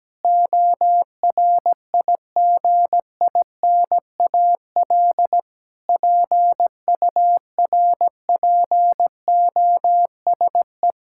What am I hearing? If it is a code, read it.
Morse code, heard as a keyed tone, about 17 wpm, 700 Hz. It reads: ORIGINAL PURPOSE
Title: Sentences from Top 1000 Words - Review Part 6 - 17wpm